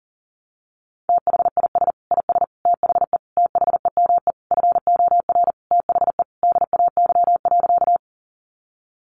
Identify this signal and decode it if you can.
Morse code, heard as a keyed tone, about 40 wpm, 700 Hz. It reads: THIS IS THE THEME FOR THE DAY.
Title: Sentences from Top 1100 Words 40wpm